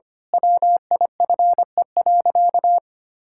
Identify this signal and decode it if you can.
Morse code, heard as a keyed tone, about 25 wpm, 700 Hz. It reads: WIFE.
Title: Pangram Challenge v2 - 25wpm